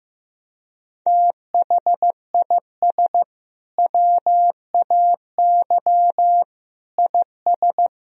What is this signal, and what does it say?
Morse code, heard as a keyed tone, about 16 wpm, 700 Hz. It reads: THIS WAY IS
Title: Sentences from Top 1000 Words - Review Part 2 - 15wpm